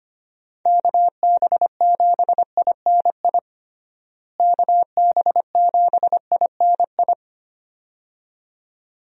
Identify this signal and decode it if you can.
Morse code, heard as a keyed tone, about 25 wpm, 700 Hz. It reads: KB7INI KB7INI
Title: USA Call Signs - 6 Characters - Repeated 2x - 25wpm